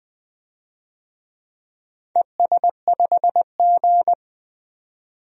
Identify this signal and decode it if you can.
Morse code, heard as a keyed tone, about 20 wpm, 700 Hz. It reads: ES5G
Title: DXCC-100 Call Signs - 4 Characters - 20wpm